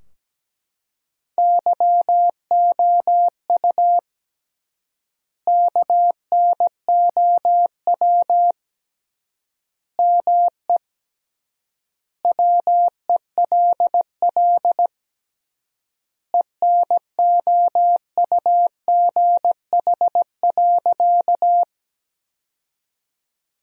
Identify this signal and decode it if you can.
Morse code, heard as a keyed tone, about 17 wpm, 700 Hz. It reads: YOU KNOW ME WELL ENOUGH.
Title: 3x Word-Spacing - Sentences from Top 400 Words - 17wpm